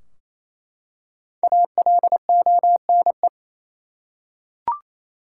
Morse code, heard as a keyed tone, about 28 wpm, 700 Hz. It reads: ALONE
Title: Five Letter Words - 28wpm